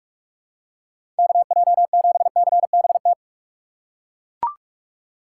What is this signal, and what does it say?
Morse code, heard as a keyed tone, about 42 wpm, 700 Hz. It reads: KJ7CBT
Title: USA Call Signs - 6 Characters - 4x Character Spacing - 45wpm